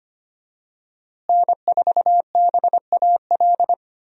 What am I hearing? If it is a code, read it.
Morse code, heard as a keyed tone, about 25 wpm, 700 Hz. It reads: N4BAL
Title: USA Call Signs - 5 Characters - Repeated 3x - 25wpm